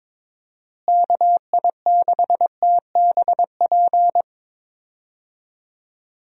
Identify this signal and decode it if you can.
Morse code, heard as a keyed tone, about 22 wpm, 700 Hz. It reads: KI6TBP
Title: USA Call Signs - 6 Characters - Repeated 2x - 22wpm